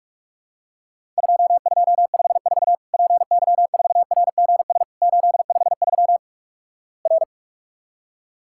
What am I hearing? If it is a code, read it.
Morse code, heard as a keyed tone, about 46 wpm, 700 Hz. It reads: 2254 PY4RGS 853 R
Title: Worked All Europe DX Contest - QTC Exchange - 45wpm